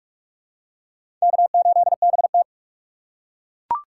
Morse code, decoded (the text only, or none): K8BT